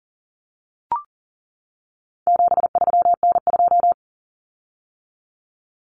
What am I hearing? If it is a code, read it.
Morse code, heard as a keyed tone, about 40 wpm, 700 Hz. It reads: 73N2
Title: Common Cut Numbers - 4-Digit 40wpm